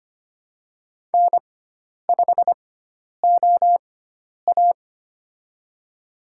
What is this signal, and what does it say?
Morse code, heard as a keyed tone, about 25 wpm, spaced as for 10 wpm, 700 Hz. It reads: N5OA
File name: Lesson 16g - Letter c - Call Signs 25wpm